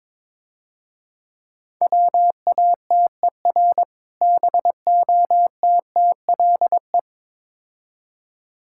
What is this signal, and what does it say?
Morse code, heard as a keyed tone, about 24 wpm, 700 Hz. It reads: WATER BOTTLE
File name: Easy Words - Speed-Racing with Hints - 22wpm